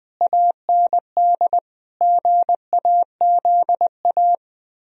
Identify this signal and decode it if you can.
Morse code, heard as a keyed tone, about 20 wpm, 700 Hz. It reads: AND GAZA
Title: Speed-Racing - Sets of 5 Words 17wpm